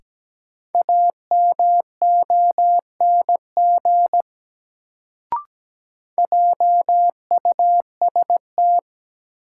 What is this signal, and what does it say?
Morse code, heard as a keyed tone, about 17 wpm, 700 Hz. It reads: AMONG JUST
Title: Sets of 2 Words - New Word Drill - 17wpm